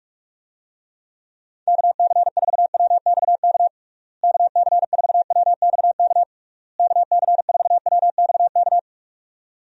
Morse code, heard as a keyed tone, about 42 wpm, 700 Hz. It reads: KC4WXK KC4WXK KC4WXK
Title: USA Call Signs - 6 Characters - Repeated 3x - 45wpm